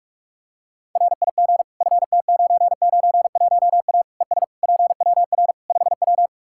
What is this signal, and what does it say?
Morse code, heard as a keyed tone, about 45 wpm, 700 Hz. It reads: RIG FT991A ES PWR 5W
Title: Practice QSO - Exchange 2 - Info - 45wpm